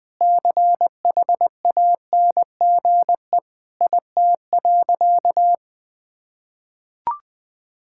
Morse code, heard as a keyed tone, about 20 wpm, 700 Hz. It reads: CHANGE IT.
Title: Sentences with 7-Word Phrases - 20wpm